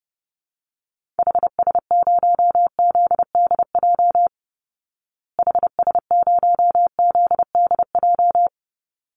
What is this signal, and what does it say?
Morse code, heard as a keyed tone, about 30 wpm, 700 Hz. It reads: HS0ZDJ HS0ZDJ
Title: Call Signs Repeated 3x - 30wpm